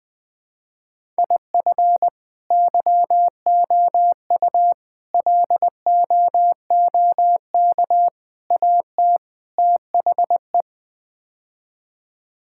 Morse code, heard as a keyed tone, about 20 wpm, 700 Hz. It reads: IF YOU LOOK AT THE
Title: Build A Phrase - 5-Word Phrases - 20wpm